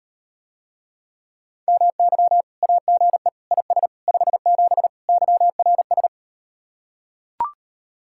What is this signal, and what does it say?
Morse code, heard as a keyed tone, about 38 wpm, 700 Hz. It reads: MY AGE IS 57 YRS